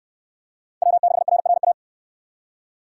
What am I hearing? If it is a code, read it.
Morse code, heard as a keyed tone, about 67 wpm, 700 Hz. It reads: W6RRA